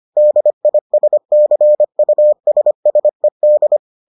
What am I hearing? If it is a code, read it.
Morse code, heard as a keyed tone, about 25 wpm, 600 Hz. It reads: DISCUSSED